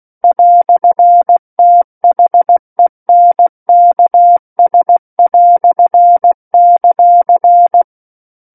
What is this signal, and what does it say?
Morse code, heard as a keyed tone, about 16 wpm, 700 Hz. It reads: "THENKS";
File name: Book One / 07-Chapter 7 - 16 WPM